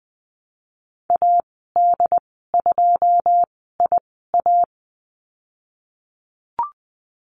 Morse code, heard as a keyed tone, about 20 wpm, spaced as for 15 wpm, 700 Hz. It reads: AD2IA
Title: 082 Lesson 14g - Number 2 - Call Signs 20wpm